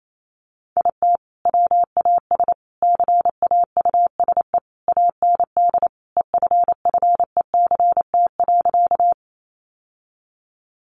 Morse code, heard as a keyed tone, about 28 wpm, 700 Hz. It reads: IT WAS CAUSE AND EFFECT.
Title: Sentences from Top 1000 Words 28wpm